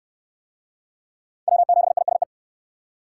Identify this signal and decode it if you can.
Morse code, heard as a keyed tone, about 67 wpm, 700 Hz. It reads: W7ISE